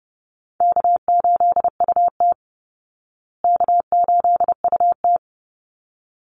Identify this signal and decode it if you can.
Morse code, heard as a keyed tone, about 30 wpm, 700 Hz. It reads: K8UT K8UT
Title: Call Signs Repeated 3x - 30wpm